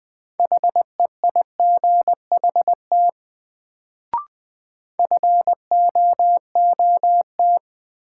Morse code, heard as a keyed tone, about 20 wpm, 700 Hz. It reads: HEIGHT FOOT
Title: Sets of 5 Words - New Word Drill - 20wpm